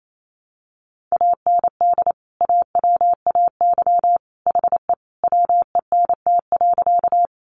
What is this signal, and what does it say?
Morse code, heard as a keyed tone, about 28 wpm, 700 Hz. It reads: AND AWAY HE WENT.
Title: Sentences from Top 600 Words 28wpm